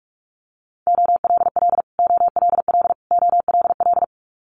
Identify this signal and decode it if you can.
Morse code, heard as a keyed tone, about 46 wpm, 700 Hz. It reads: OLL OLL OLL
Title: Multi-letter Phonemes Repeated 3x 45wpm